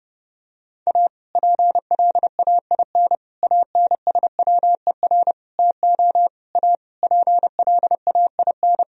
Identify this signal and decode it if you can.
Morse code, heard as a keyed tone, about 32 wpm, 700 Hz. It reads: A PLAIN ANSWER TO A PLAIN
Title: Sentences from Top 1000 Words - Review Part 2 - 30wpm